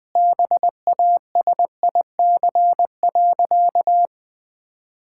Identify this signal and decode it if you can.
Morse code, heard as a keyed tone, about 21 wpm, 700 Hz. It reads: BASIC.
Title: Sentences with 3-Word Phrases - Spoken Beginning - 20wpm